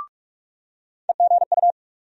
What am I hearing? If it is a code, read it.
Morse code, heard as a keyed tone, about 46 wpm, 700 Hz. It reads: EGU
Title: USA Call Sign Postfixes - 3 Letters 45wpm